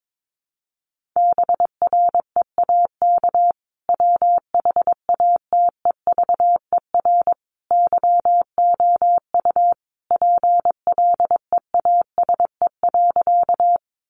Morse code, heard as a keyed tone, about 22 wpm, 700 Hz. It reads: BREAK WHATEVER YOU PLEASE.
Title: Sentences from Top 1500 Words 22wpm